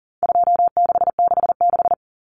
Morse code, heard as a keyed tone, about 40 wpm, 700 Hz. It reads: 2666
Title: Common Cut Numbers - 4-Digit 40wpm